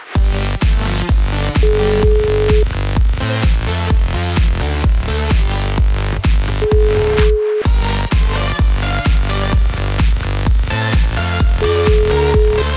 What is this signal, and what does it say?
Ringback tone (Europe, 425 Hz), with noise.